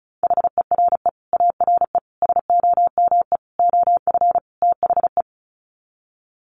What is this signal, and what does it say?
Morse code, heard as a keyed tone, about 34 wpm, 700 Hz. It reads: HERE ARE SOME OF THE
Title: Sets of 5 Words Encore 35wpm